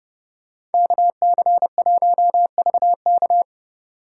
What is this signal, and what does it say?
Morse code, heard as a keyed tone, about 29 wpm, 700 Hz. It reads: KC1VK